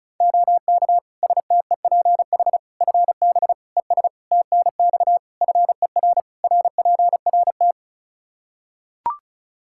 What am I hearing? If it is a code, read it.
Morse code, heard as a keyed tone, about 36 wpm, 700 Hz. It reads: OK STEPH FB ES TNX FER RPRT
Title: Practice QSO - Exchange 2 - Info - 35wpm